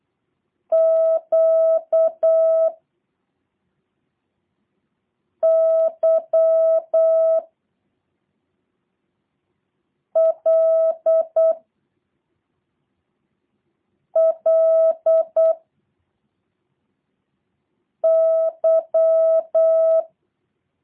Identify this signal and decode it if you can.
Morse code, heard as a keyed tone, about 8 wpm, 660 Hz. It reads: Q Y L L Y
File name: jh8cbh / 2023cw / review1-6